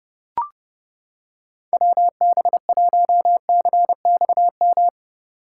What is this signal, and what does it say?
Morse code, heard as a keyed tone, about 29 wpm, 700 Hz. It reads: WB1CXM